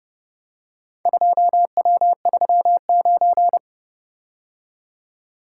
Morse code, heard as a keyed tone, about 30 wpm, 700 Hz. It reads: 2W39